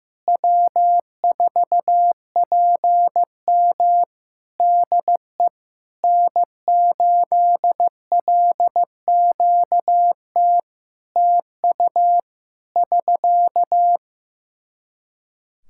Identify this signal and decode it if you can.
Morse code, heard as a keyed tone, about 15 wpm, 700 Hz. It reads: W4PM DE N8LQT TU <SK>
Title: Practice QSO - Exchange 4 - Ending - 15wpm